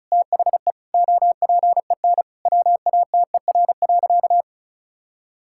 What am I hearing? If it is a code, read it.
Morse code, heard as a keyed tone, about 36 wpm, 700 Hz. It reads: THE OPEN WATER.
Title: Sentences from Top 1000 Words - Review Part 6 - 35wpm